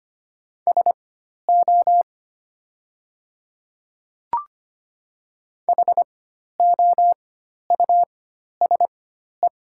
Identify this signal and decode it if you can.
Morse code, heard as a keyed tone, about 25 wpm, spaced as for 12 wpm, 700 Hz. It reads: SO HOUSE
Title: 4x Character Spacing - Top 200 Words - 25wpm